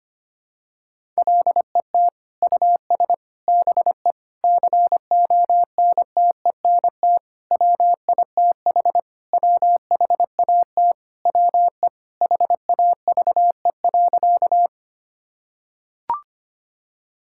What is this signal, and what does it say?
Morse code, heard as a keyed tone, about 25 wpm, 700 Hz. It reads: LET US BE CONTENT WITH WHAT WE HAVE.
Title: Sentences from Top 1000 Words - Review Part 6 - 25wpm